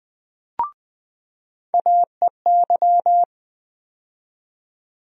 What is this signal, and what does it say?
Morse code, heard as a keyed tone, about 20 wpm, 700 Hz. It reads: AEY
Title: USA Call Sign Postfixes - 3 Letters 20wpm